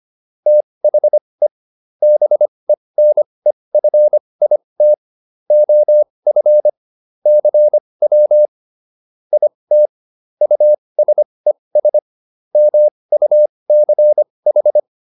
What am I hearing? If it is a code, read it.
Morse code, heard as a keyed tone, about 25 wpm, spaced as for 20 wpm, 600 Hz. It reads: THE BENEFIT OF CW IT USES MUCH